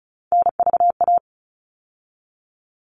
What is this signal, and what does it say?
Morse code, heard as a keyed tone, about 34 wpm, 700 Hz. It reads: NVA
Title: All Cut Numbers - 3-Digit 35wpm